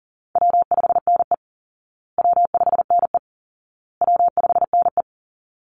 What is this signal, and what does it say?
Morse code, heard as a keyed tone, about 40 wpm, 700 Hz. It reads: W5NE W5NE W5NE
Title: Easy Call Signs Repeated 3x - 40wpm